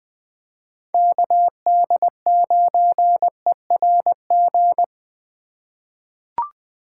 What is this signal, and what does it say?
Morse code, heard as a keyed tone, about 20 wpm, 700 Hz. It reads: KD9ERG
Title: USA Call Signs - 6 Characters - 4x Character Spacing - 20wpm